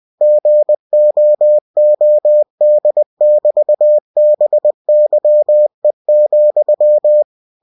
Morse code, heard as a keyed tone, about 20 wpm, 600 Hz. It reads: GOOD<BT>BYE,
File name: cw-books / winnie-the-pooh / winnie-20wpm-page-0070